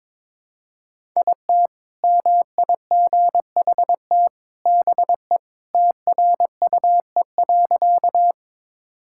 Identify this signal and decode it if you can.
Morse code, heard as a keyed tone, about 21 wpm, 700 Hz. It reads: IT MIGHT BE TRUE.
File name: practice / 22wpm / Sentences with 2-Word Phrases - 22wpm